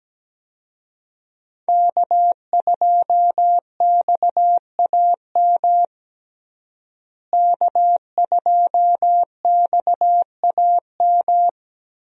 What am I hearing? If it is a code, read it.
Morse code, heard as a keyed tone, about 17 wpm, 700 Hz. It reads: K2XAM K2XAM
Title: USA Call Signs - 5 Characters - Repeated 3x - 17wpm